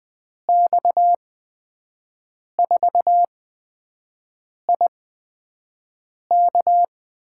Morse code, heard as a keyed tone, about 20 wpm, spaced as for 5 wpm, 700 Hz. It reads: X4IK
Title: USA Call Signs - 5 Characters - 8x Character Spacing - 20wpm